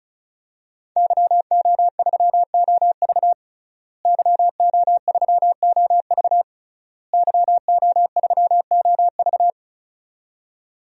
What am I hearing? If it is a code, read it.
Morse code, heard as a keyed tone, about 34 wpm, 700 Hz. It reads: YO3OV YO3OV YO3OV
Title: DXCC-100 Call Signs - 5 Characters - Repeated 3x - 35wpm